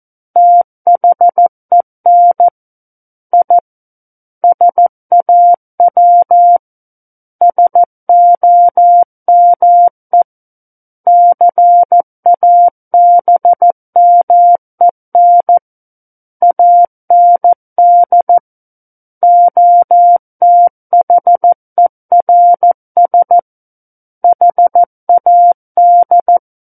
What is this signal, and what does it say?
Morse code, heard as a keyed tone, about 14 wpm, 700 Hz. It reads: THEN I SAW SOME CABMEN AND OTHERS HAD
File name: Book One / 05-Chapter 5 - 14 WPM